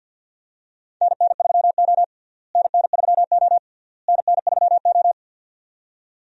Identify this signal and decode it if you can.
Morse code, heard as a keyed tone, about 46 wpm, 700 Hz. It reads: NN3O NN3O NN3O
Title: USA Call Signs - 4 Characters - Repeated 3x - 50wpm